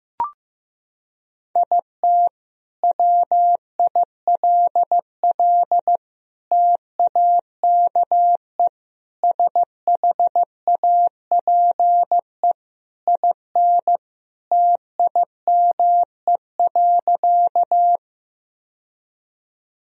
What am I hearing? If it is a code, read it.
Morse code, heard as a keyed tone, about 16 wpm, 700 Hz. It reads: IT WILL TAKE SHAPE IN TIME.
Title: Sentences from Top 1000 Words - Review Part 3 - 15wpm